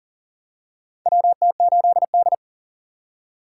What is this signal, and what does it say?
Morse code, heard as a keyed tone, about 40 wpm, 700 Hz. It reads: WT8D
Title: USA Call Signs - 4 Characters - 6x Character Spacing - 40wpm